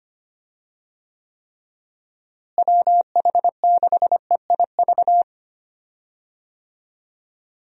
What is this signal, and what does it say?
Morse code, heard as a keyed tone, about 25 wpm, 700 Hz. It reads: WH6EIV
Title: USA Call Signs - 6 Characters - Repeated 3x - 25wpm